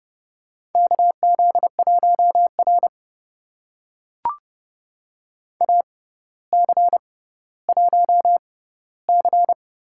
Morse code, heard as a keyed tone, about 30 wpm, spaced as for 15 wpm, 700 Hz. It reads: KZ1R AC1C
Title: USA Call Signs - 4 Characters - 6x Character Spacing - 30wpm